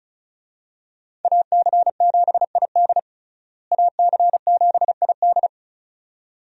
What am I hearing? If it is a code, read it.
Morse code, heard as a keyed tone, about 36 wpm, 700 Hz. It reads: AC7ID AC7ID
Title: USA Call Signs - 5 Characters - Repeated 2x - 35wpm